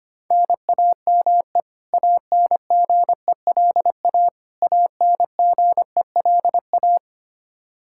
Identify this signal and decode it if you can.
Morse code, heard as a keyed tone, about 25 wpm, 700 Hz. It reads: NAME ANGELA ANGELA
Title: Practice QSO - Exchange 1 - Beginning - 25wpm